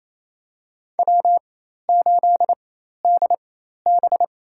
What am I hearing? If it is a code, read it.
Morse code, heard as a keyed tone, about 27 wpm, spaced as for 13 wpm, 700 Hz. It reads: W8DB
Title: USA Call Signs - 4x Character Spacing - 28wpm